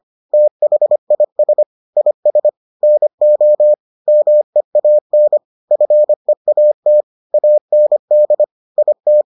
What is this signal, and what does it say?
Morse code, heard as a keyed tone, about 25 wpm, 600 Hz. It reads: THIS IS NO MEAN FEAT AND IT